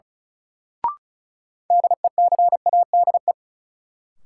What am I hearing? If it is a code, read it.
Morse code, heard as a keyed tone, about 36 wpm, 700 Hz. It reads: DECADE
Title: Sets of 5 Words - New Word Drill - 35wpm